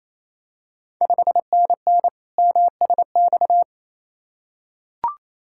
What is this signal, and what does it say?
Morse code, heard as a keyed tone, about 28 wpm, 700 Hz. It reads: 5NN MSX
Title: Ontario QSO Party Exchange 28wpm